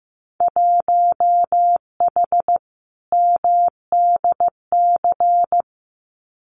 Morse code, heard as a keyed tone, about 15 wpm, 700 Hz. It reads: 1H MDC
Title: Winter Field Day Exchanges 15wpm